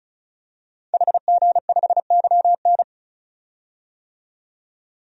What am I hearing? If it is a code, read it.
Morse code, heard as a keyed tone, about 36 wpm, 700 Hz. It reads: HG5YN
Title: DXCC-100 Call Signs - 5 Characters - 35wpm